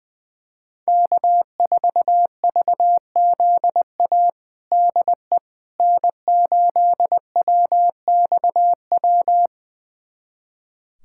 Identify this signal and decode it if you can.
Morse code, heard as a keyed tone, about 20 wpm, 700 Hz. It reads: K4VZA DE N8WXW
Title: USA Callsign DE Callsign - 5 Characters - 20wpm